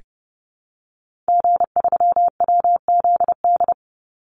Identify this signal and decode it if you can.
Morse code, heard as a keyed tone, about 30 wpm, 700 Hz. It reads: G3WZD
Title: Call Signs Repeated 2x - 30wpm